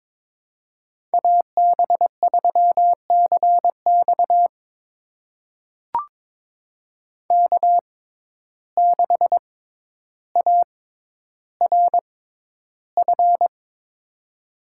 Morse code, heard as a keyed tone, about 22 wpm, spaced as for 11 wpm, 700 Hz. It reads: AB3CX K6ARF
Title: USA Call Signs - 5 Characters - 6x Character Spacing - 22wpm